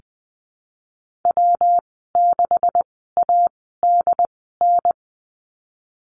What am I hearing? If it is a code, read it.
Morse code, heard as a keyed tone, about 19 wpm, spaced as for 14 wpm, 700 Hz. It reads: W6ADN
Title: 179 Lesson 29g - Slash - Call Signs 20wpm